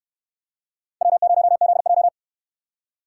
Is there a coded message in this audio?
Yes: Morse code from 1.0 to 2.1 s, heard as a keyed tone, about 67 wpm, 700 Hz. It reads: W0ZJ